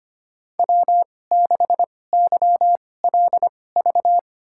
Morse code, heard as a keyed tone, about 25 wpm, spaced as for 18 wpm, 700 Hz. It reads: W6YLV
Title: USA Call Signs - 5 Characters - 2x Character Spacing - 25wpm